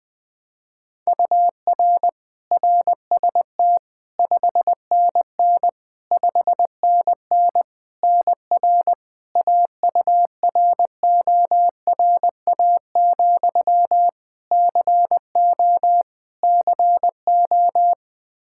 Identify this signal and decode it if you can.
Morse code, heard as a keyed tone, about 21 wpm, 700 Hz. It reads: UR RST 5NN 5NN NR AURORA, CO CO
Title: Practice QSO - Exchange 1 - Beginning - 20wpm